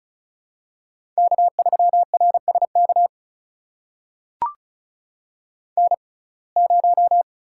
Morse code, heard as a keyed tone, about 34 wpm, spaced as for 25 wpm, 700 Hz. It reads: K3RSK N0
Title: USA Call Signs - 5 Characters - 6x Character Spacing - 35wpm